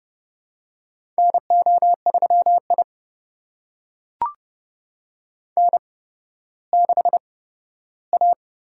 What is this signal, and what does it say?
Morse code, heard as a keyed tone, about 31 wpm, spaced as for 15 wpm, 700 Hz. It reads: NO3I N6A
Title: USA Call Signs - 4 Characters - 8x Character Spacing - 30wpm